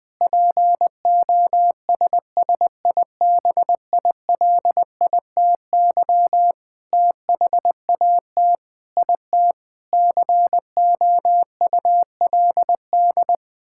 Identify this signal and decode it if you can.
Morse code, heard as a keyed tone, about 20 wpm, 700 Hz. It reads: POSSIBILITY THAT IT COULD
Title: Sentences with 9-Word Phrases - Spoken Beginning - 20wpm